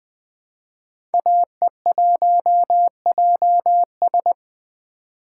Morse code, heard as a keyed tone, about 20 wpm, 700 Hz. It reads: AE1JS
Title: 2023 SOTA Chaser Call Signs - 20wpm